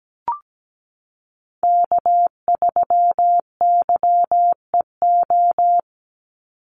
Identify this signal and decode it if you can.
Morse code, heard as a keyed tone, about 17 wpm, 700 Hz. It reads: K3YEO
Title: Call Signs - 17wpm